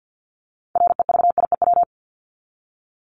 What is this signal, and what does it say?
Morse code, heard as a keyed tone, about 50 wpm, 700 Hz. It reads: REVIEW
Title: Top 751-1000 Words 50wpm